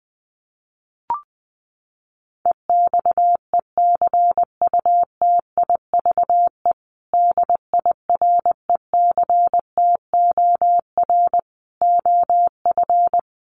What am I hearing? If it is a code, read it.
Morse code, heard as a keyed tone, about 20 wpm, 700 Hz. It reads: EXECUTIVE DIRECTOR OF
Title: Sets of 5 Words Encore2 20wpm